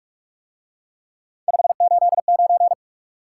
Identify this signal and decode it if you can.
Morse code, heard as a keyed tone, about 44 wpm, 700 Hz. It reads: 589